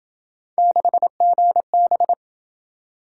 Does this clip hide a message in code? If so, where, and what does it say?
Yes: Morse code from 0.6 to 2.1 s, heard as a keyed tone, about 26 wpm, 700 Hz. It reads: 6GB